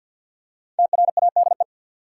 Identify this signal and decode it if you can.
Morse code, heard as a keyed tone, about 47 wpm, 700 Hz. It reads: TRADE
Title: Five Letter Words - 50wpm